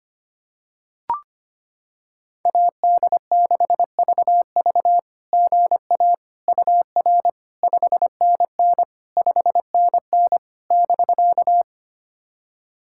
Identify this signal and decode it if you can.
Morse code, heard as a keyed tone, about 25 wpm, 700 Hz. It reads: AD6VV GA UR 5NN 5NN <BK>
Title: SOTA QSO Practice 25wpm